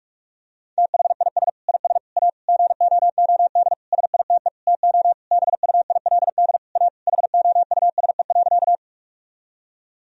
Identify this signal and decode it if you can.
Morse code, heard as a keyed tone, about 46 wpm, 700 Hz. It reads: THIS IS A GOOD SITE TO BUILD A HOUSE.
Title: Sentences from Top 1000 Words - Review Part 4 - 45wpm